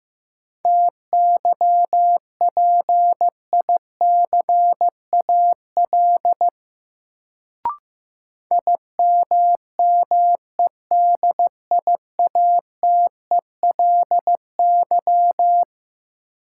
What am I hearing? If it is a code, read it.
Morse code, heard as a keyed tone, about 15 wpm, 700 Hz. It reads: TYPICAL IMMEDIATELY
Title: Sentences from Top 1100 Words - New Word Drill - 15wpm